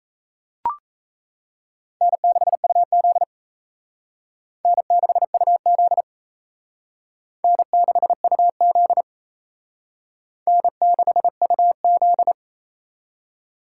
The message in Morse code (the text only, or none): N6UZ N6UZ N6UZ N6UZ